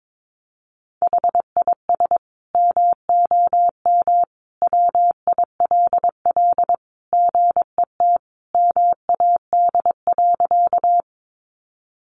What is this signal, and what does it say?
Morse code, heard as a keyed tone, about 22 wpm, 700 Hz. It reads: HIS MOM WILL GET MAD.
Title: Additional Easy Phrases 22wpm